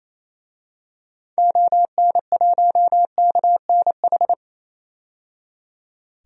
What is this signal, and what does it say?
Morse code, heard as a keyed tone, about 28 wpm, 700 Hz. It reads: ON1KNH
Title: DXCC-100 Call Signs - 6 Characters - 28wpm